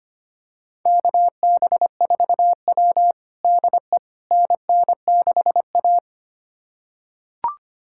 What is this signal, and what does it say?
Morse code, heard as a keyed tone, about 25 wpm, 700 Hz. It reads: KB4W DE NN6A
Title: USA Callsign DE Callsign - 4 Characters - 25wpm